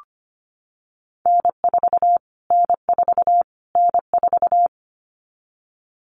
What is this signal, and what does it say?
Morse code, heard as a keyed tone, about 25 wpm, 700 Hz. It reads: N4 N4 N4
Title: Callsign Prefixes for Top 25 DXCC - Repeated 3x - 25wpm